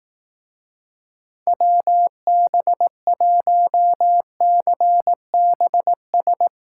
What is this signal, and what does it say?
Morse code, heard as a keyed tone, about 19 wpm, 700 Hz. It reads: WB1CBS
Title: Speed Racing - USA Call Signs 15wpm